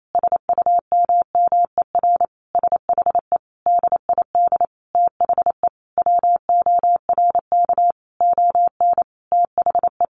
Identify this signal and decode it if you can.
Morse code, heard as a keyed tone, about 28 wpm, 700 Hz. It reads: SUMMER SHE DID THE WORK ON THE
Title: Sentences from Top 1200 Words 28wpm